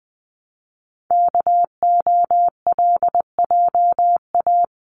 Morse code, heard as a keyed tone, about 20 wpm, 700 Hz. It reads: KOLJA
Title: CWT Exchange 20wpm